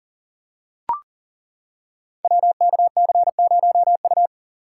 Morse code, heard as a keyed tone, about 39 wpm, 700 Hz. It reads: WKC0U
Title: Code Groups - 5 Letter:Numbers - 40wpm